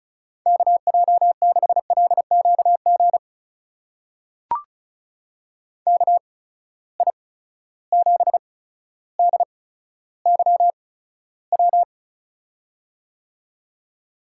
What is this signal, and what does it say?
Morse code, heard as a keyed tone, about 36 wpm, spaced as for 15 wpm, 700 Hz. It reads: KJ6LQG KI7DYW
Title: USA Call Signs - 6 Characters - 8x Character Spacing - 35wpm